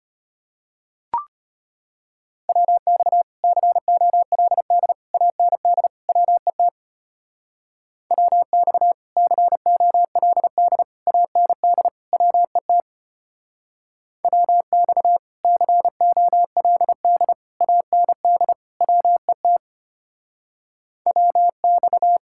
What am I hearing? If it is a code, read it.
Morse code, heard as a keyed tone, about 31 wpm, 700 Hz. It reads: WX COLD AND WET WX COLD AND WET WX COLD AND WET WX
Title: Speed-Racing Common QSO Phrases - 25wpm